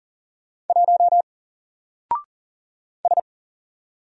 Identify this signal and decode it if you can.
Morse code, heard as a keyed tone, about 40 wpm, 700 Hz. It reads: 1 S
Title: Single Character (Letter:Number:Punctuation:Prosign) 40wpm